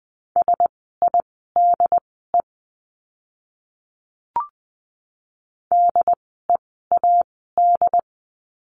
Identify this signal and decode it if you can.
Morse code, heard as a keyed tone, about 20 wpm, spaced as for 14 wpm, 700 Hz. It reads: SIDE DEAD
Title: 075 Lesson 13f - Letter L - Words 20wpm